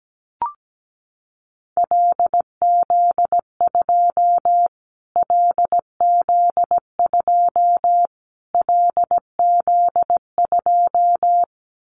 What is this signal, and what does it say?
Morse code, heard as a keyed tone, about 17 wpm, 700 Hz. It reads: LZ2 LZ2 LZ2
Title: Callsign Prefixes for Top 25 DXCC - Repeated 3x - 17wpm